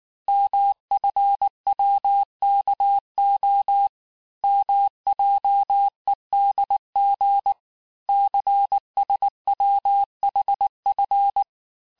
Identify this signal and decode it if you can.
Morse code, heard as a keyed tone, about 19 wpm, 800 Hz. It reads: MFWKO MJEDG CSWHF
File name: cours-du-lundi files / cours-1595-teste-CHIFFRE-vitesse18-MPM